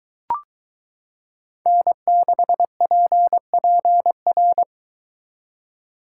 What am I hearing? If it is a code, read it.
Morse code, heard as a keyed tone, about 22 wpm, 700 Hz. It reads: N6PPR